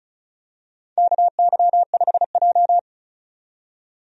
Morse code, heard as a keyed tone, about 36 wpm, 700 Hz. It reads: KY5J
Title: USA Call Signs - 4 Characters - 35wpm